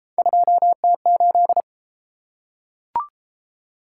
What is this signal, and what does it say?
Morse code, heard as a keyed tone, about 32 wpm, 700 Hz. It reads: 2T8